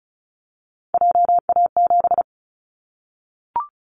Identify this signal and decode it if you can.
Morse code, heard as a keyed tone, about 36 wpm, 700 Hz. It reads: JA7
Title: Callsign Prefixes for Top 50 DXCC - 35wpm